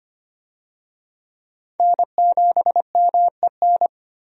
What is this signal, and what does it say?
Morse code, heard as a keyed tone, about 25 wpm, 700 Hz. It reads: N7MEN
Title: USA Call Signs - 5 Characters - 25wpm